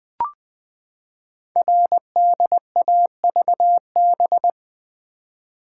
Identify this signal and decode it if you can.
Morse code, heard as a keyed tone, about 20 wpm, 700 Hz. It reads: RDAVB
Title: Code Groups - 5 Letters - 20wpm